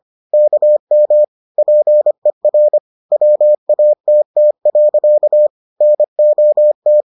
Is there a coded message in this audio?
Yes: Morse code from 0.3 to 7.0 s, heard as a keyed tone, about 25 wpm, 600 Hz. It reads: KM PER WATT. NOT